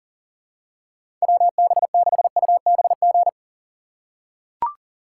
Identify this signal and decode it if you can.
Morse code, heard as a keyed tone, about 40 wpm, 700 Hz. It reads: WB6UBG
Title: USA Call Signs - 6x Character Spacing - 40wpm